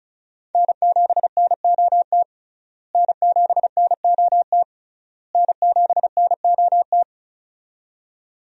Morse code, heard as a keyed tone, about 34 wpm, 700 Hz. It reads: N7NOT N7NOT N7NOT
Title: USA Call Signs - 5 Characters - Repeated 3x - 35wpm